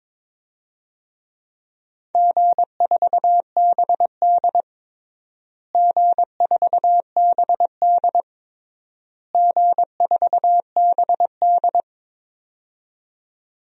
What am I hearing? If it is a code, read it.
Morse code, heard as a keyed tone, about 22 wpm, 700 Hz. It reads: G4BD G4BD G4BD
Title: DXCC-100 Call Signs - 4 Characters - Repeated 3x - 22wpm